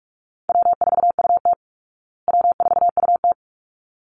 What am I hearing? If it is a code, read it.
Morse code, heard as a keyed tone, about 44 wpm, 700 Hz. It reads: W4UT W4UT
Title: Call Signs Repeated 3x - 45wpm